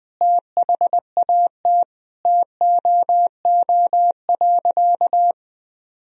Morse code, heard as a keyed tone, about 20 wpm, 700 Hz. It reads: THAT TOO.
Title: Sentences from Top 1000 Words - Review Part 1 - 20wpm